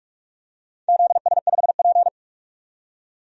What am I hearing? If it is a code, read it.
Morse code, heard as a keyed tone, about 42 wpm, 700 Hz. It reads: ZS5P
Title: DXCC-100 Call Signs - 4 Characters - 45wpm